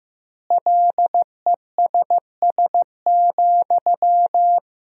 Morse code, heard as a keyed tone, about 15 wpm, 700 Hz. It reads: LESS,
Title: 3x Word-Spacing - Sentences from Top 400 Words - 15wpm